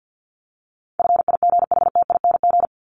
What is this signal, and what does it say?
Morse code, heard as a keyed tone, about 48 wpm, 700 Hz. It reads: FIGHTING
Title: ING Suffix - Top 100 Words with Base Words 50wpm